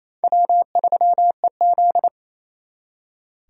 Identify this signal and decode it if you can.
Morse code, heard as a keyed tone, about 35 wpm, 700 Hz. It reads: W3EZ
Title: Speed Racing - USA Call Signs 28wpm